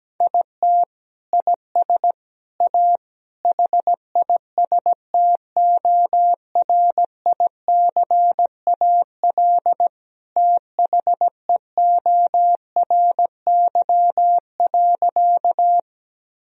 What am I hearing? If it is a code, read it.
Morse code, heard as a keyed tone, about 17 wpm, 700 Hz. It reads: IT IS A HISTORICAL THEORY.
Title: Sentences from Top 1000 Words - Review Part 4 - 17wpm